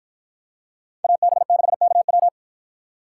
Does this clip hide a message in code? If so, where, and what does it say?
Yes: Morse code from 1.0 to 2.3 s, heard as a keyed tone, about 52 wpm, 700 Hz. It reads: AB6KW